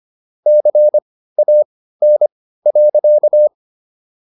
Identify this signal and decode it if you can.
Morse code, heard as a keyed tone, about 25 wpm, spaced as for 15 wpm, 600 Hz. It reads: CAN.